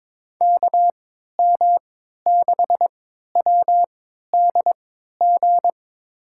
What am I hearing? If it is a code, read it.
Morse code, heard as a keyed tone, about 22 wpm, spaced as for 13 wpm, 700 Hz. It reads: KM6WDG
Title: USA Call Signs - 6 Characters - 3x Character Spacing - 22wpm